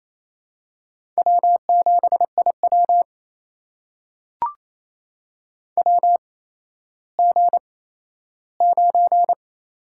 Morse code, heard as a keyed tone, about 28 wpm, spaced as for 14 wpm, 700 Hz. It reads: W7IW WG9
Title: USA Call Signs - 4 Characters - 8x Character Spacing - 28wpm